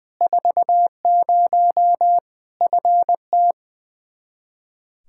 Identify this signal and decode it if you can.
Morse code, heard as a keyed tone, about 20 wpm, 700 Hz. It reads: 40 FT
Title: K1USN SST QSO Practice 20wpm